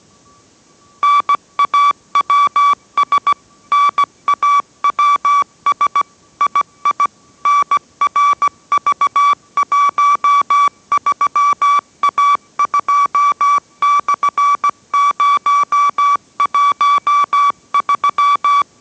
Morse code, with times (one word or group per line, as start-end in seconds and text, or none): 1.0-3.3 NAWS
3.7-6.0 NAWS
6.4-7.1 II
7.5-18.6 NRV13A2/013